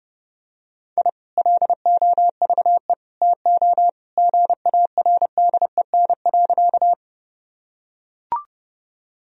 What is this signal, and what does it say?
Morse code, heard as a keyed tone, about 30 wpm, 700 Hz. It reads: I LOVE TO GARDEN.
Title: Sentences from Top 1000 Words - Review Part 4 - 30wpm